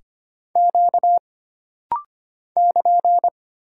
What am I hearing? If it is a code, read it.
Morse code, heard as a keyed tone, about 25 wpm, 700 Hz. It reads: Q <KN>
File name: Single Character (Letter:Number:Punctuation:Prosign) - Rapid-Fire - 25wpm